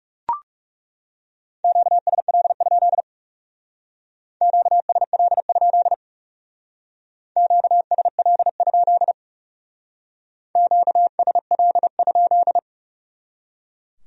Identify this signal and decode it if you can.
Morse code, heard as a keyed tone, about 38 wpm, 700 Hz. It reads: QSL? QSL? QSL? QSL?